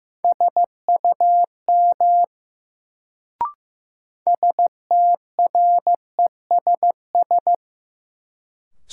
Morse code, heard as a keyed tone, about 16 wpm, 700 Hz. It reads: SUM STRESS
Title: Sentences from Top 1000 Words - Review Part 6 - New Word Drill - 15wpm